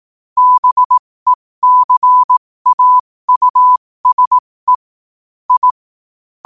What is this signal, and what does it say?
Morse code, heard as a keyed tone, about 18 wpm, spaced as for 15 wpm, 1000 Hz. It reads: BECAUSE I
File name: qotdfiles / 15wpm